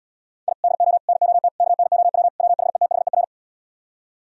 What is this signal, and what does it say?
Morse code, heard as a keyed tone, about 78 wpm, 700 Hz. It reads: I DO NOT KNOW WHETHER